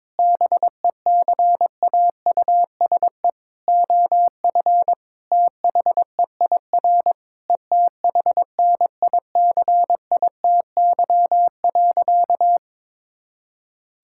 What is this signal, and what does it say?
Morse code, heard as a keyed tone, about 22 wpm, 700 Hz. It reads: BECAUSE OF THEIR ETHNICITY.
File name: Sentences with 9-Word Phrases - 22wpm